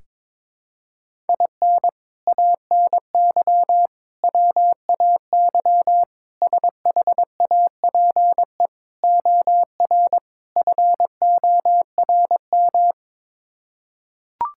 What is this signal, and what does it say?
Morse code, heard as a keyed tone, about 22 wpm, 700 Hz. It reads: IN ANY WAY SHAPE OR FORM
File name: practice / 22wpm / Sets of 6 Words - 22wpm